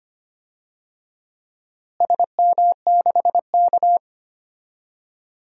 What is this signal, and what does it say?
Morse code, heard as a keyed tone, about 25 wpm, 700 Hz. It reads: SM6K
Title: DXCC-100 Call Signs - 4 Characters - Repeated 2x - 25wpm